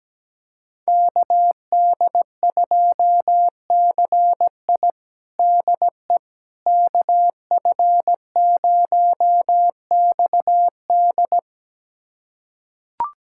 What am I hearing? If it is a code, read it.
Morse code, heard as a keyed tone, about 17 wpm, 700 Hz. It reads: KD2CI DE KF0XD
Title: USA Callsign DE Callsign - 5 Characters - 17wpm